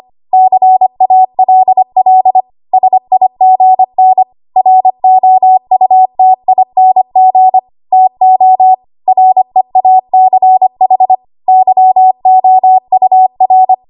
Morse code, heard as a keyed tone, about 25 wpm, 750 Hz. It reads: CALL SIGN ROUTING TO REACH YOUR
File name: Archive / 25 WPM / 25WPM